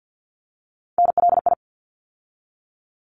Morse code, heard as a keyed tone, about 50 wpm, 700 Hz. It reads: NLI
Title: ARRL Sections 50wpm